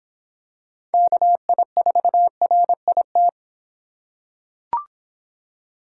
Morse code, heard as a keyed tone, about 24 wpm, 700 Hz. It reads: KI4RIT